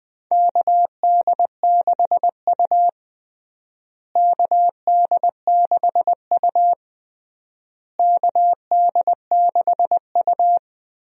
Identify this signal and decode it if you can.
Morse code, heard as a keyed tone, about 20 wpm, 700 Hz. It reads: KD6U KD6U KD6U
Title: USA Call Signs - 4 Characters - Repeated 3x - 20wpm